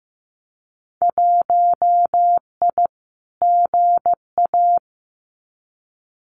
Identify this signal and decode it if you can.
Morse code, heard as a keyed tone, about 15 wpm, 700 Hz. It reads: 1I GA
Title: Winter Field Day Exchanges Encore 15wpm